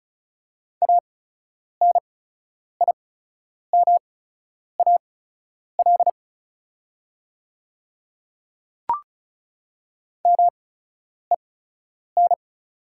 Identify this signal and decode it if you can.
Morse code, heard as a keyed tone, about 34 wpm, spaced as for 10 wpm, 690 Hz. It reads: ANIMAL MEN
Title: 8x Character Spacing - Top 200 Words - 35wpm